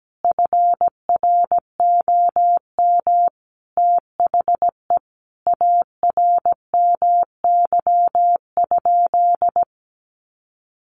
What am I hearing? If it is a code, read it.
Morse code, heard as a keyed tone, about 17 wpm, 700 Hz. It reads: FROM THE ARMY?
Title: Sentences from Top 800 Words 17wpm